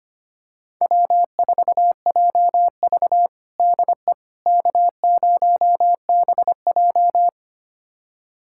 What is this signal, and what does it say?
Morse code, heard as a keyed tone, about 25 wpm, 700 Hz. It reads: W4JV DE K0BJ
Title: USA Callsign DE Callsign - 4 Characters - 25wpm